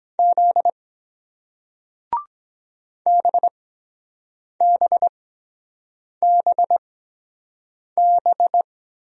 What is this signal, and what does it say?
Morse code, heard as a keyed tone, about 24 wpm, 700 Hz. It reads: Z B B B B